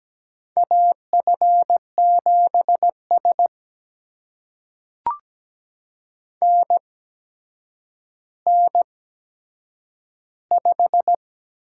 Morse code, heard as a keyed tone, about 17 wpm, spaced as for 8 wpm, 700 Hz. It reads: AF7S NN5
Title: USA Call Signs - 4 Characters - 8x Character Spacing - 17wpm